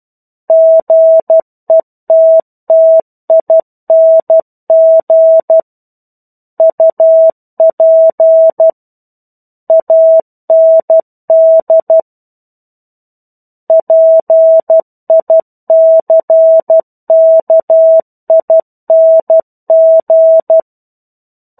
Morse code, heard as a keyed tone, about 12 wpm, 650 Hz. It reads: GETTING UP AND PICKING
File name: 01-Chapter-01 12 WPM